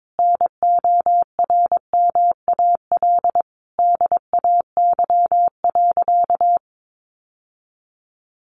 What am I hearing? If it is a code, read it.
Morse code, heard as a keyed tone, about 22 wpm, 700 Hz. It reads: NORMAL DAY.
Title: Sentences from Top 1200 Words 22wpm